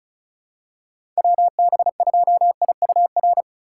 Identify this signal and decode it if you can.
Morse code, heard as a keyed tone, about 36 wpm, 700 Hz. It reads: WB2IUR
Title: USA Call Signs - 6 Characters - 3x Character Spacing - 35wpm